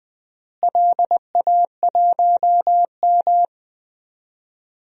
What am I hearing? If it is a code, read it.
Morse code, heard as a keyed tone, about 20 wpm, 700 Hz. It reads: LA1M
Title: DXCC-100 Call Signs - 4 Characters - Repeated 3x - 20wpm